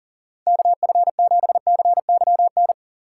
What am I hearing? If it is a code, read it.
Morse code, heard as a keyed tone, about 41 wpm, 700 Hz. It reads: KF7CYN
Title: USA Call Signs - 6 Characters - 8x Character Spacing - 40wpm